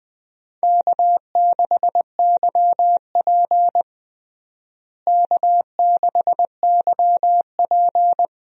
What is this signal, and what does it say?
Morse code, heard as a keyed tone, about 20 wpm, 700 Hz. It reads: K6YP K6YP
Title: USA Call Signs - 4 Characters - Repeated 2x - 20wpm